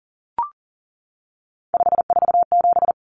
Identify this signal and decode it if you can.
Morse code, heard as a keyed tone, about 40 wpm, 700 Hz. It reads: 547
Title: Common Cut Numbers - 3-Digit 40wpm